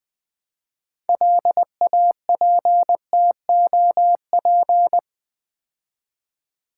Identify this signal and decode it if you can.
Morse code, heard as a keyed tone, about 20 wpm, 700 Hz. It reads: LAPTOP
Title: Build A Word - 3 Letters At A Time - 6-Letter Words - 20wpm